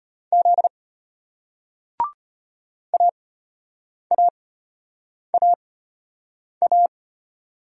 Morse code, heard as a keyed tone, about 33 wpm, spaced as for 8 wpm, 700 Hz. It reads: Z AAAA